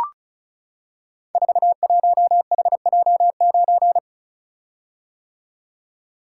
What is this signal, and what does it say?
Morse code, heard as a keyed tone, about 36 wpm, 700 Hz. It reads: 41HJ9
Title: Code Groups - 5 Letter:Numbers - 35wpm